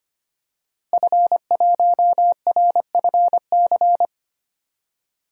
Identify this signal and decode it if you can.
Morse code, heard as a keyed tone, about 25 wpm, 700 Hz. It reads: F1RFC
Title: DXCC-100 Call Signs - 5 Characters - Repeated 3x - 25wpm